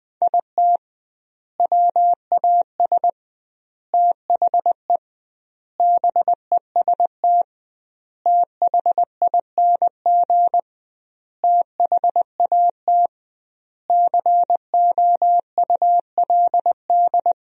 Morse code, heard as a keyed tone, about 20 wpm, 700 Hz. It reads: IT WAS THE BEST THING THAT COULD
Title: 2x Word-Spacing - Sentences from Top 400 Words - 20wpm